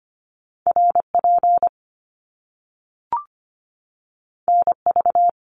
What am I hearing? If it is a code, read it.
Morse code, heard as a keyed tone, about 25 wpm, 700 Hz. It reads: RP NV
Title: Top Two Letter Combinations in Words 25wpm